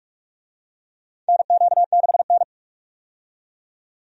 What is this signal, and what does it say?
Morse code, heard as a keyed tone, about 46 wpm, 700 Hz. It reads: NQ6N
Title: USA Call Signs - 4 Characters - 45wpm